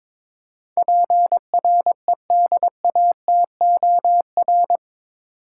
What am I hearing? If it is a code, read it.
Morse code, heard as a keyed tone, about 22 wpm, 700 Hz. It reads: PREDATOR
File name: practice / 22wpm / Eight Letter Words - 22wpm